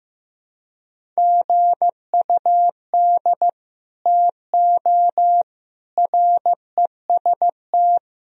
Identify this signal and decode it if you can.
Morse code, heard as a keyed tone, about 15 wpm, 700 Hz. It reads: GUD TO REST
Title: Sentences from Top 500 Words and Abbreviations 15wpm